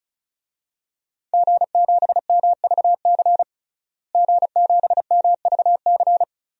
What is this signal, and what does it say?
Morse code, heard as a keyed tone, about 34 wpm, 700 Hz. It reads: G7MVC G7MVC
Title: DXCC-100 Call Signs - 5 Characters - Repeated 3x - 35wpm